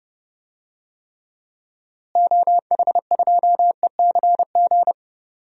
Morse code, heard as a keyed tone, about 30 wpm, 700 Hz. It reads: OH2ECG
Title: DXCC-100 Call Signs - 6 Characters - Repeated 3x - 30wpm